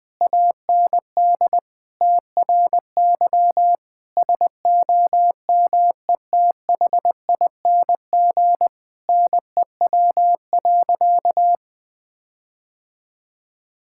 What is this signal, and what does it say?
Morse code, heard as a keyed tone, about 20 wpm, 700 Hz. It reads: AND TRY SOMETHING NEW.
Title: Sentences with 8-Word Phrases - 20wpm